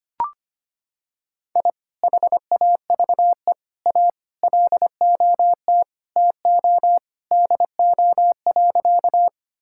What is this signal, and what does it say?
Morse code, heard as a keyed tone, about 25 wpm, 700 Hz. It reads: I HAVE A LOT TO DO.
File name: practice / 25wpm / Sentences with 2-Word Phrases - 25wpm